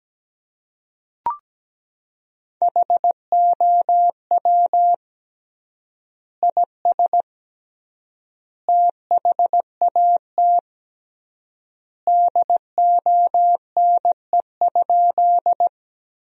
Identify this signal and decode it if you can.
Morse code, heard as a keyed tone, about 17 wpm, 700 Hz. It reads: HOW IS THAT DONE?
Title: 3x Word-Spacing - Sentences from Top 500 Words - 17wpm